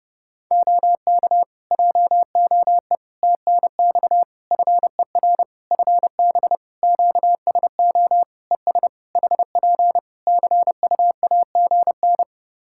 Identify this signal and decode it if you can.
Morse code, heard as a keyed tone, about 30 wpm, 700 Hz. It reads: OK JOE TNX FER FB QSO ES HP CUAGN
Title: K1USN SST QSO Practice 30wpm